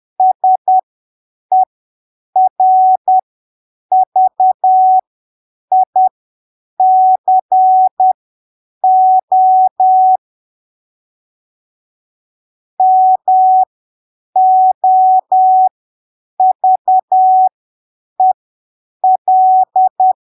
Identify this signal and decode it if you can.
Morse code, heard as a keyed tone, about 10 wpm, spaced as for 7 wpm, 750 Hz. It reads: SERVICO MOVEL
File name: arquivos / texto / texto 07 10ppm